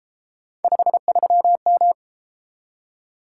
Morse code, heard as a keyed tone, about 30 wpm, 700 Hz. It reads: 53M